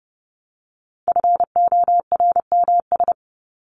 Morse code, heard as a keyed tone, about 30 wpm, 700 Hz. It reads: FORMS
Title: ES:S Suffix - Top 100 Words with Base Words 30wpm